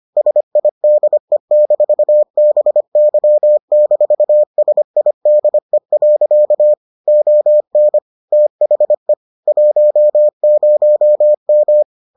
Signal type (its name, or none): Morse code